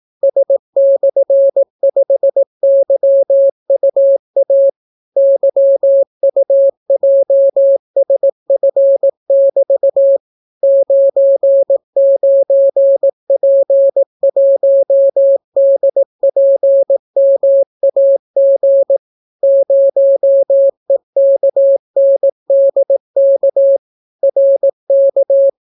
Morse code, heard as a keyed tone, about 18 wpm, 550 Hz. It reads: S/5YUA YUJSF<BT> 99P1DPMAG 0EKNDK RK